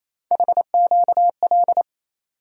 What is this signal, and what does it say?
Morse code, heard as a keyed tone, about 28 wpm, 700 Hz. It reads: HQL
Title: USA Call Sign Postfixes - 3 Letters 28wpm